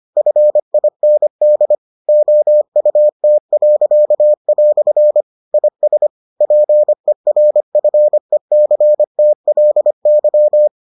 Morse code, heard as a keyed tone, about 25 wpm, 600 Hz. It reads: FIND OUT." IS PERFECTLY